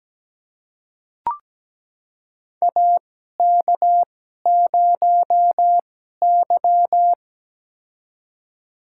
Morse code, heard as a keyed tone, about 17 wpm, spaced as for 12 wpm, 700 Hz. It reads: AK0Y
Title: USA Call Signs - 4 Characters - 2x Character Spacing - 17wpm